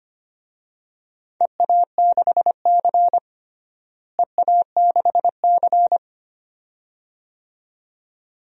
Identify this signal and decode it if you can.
Morse code, heard as a keyed tone, about 25 wpm, 700 Hz. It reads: EA6C EA6C
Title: DXCC-100 Call Signs - 4 Characters - Repeated 2x - 25wpm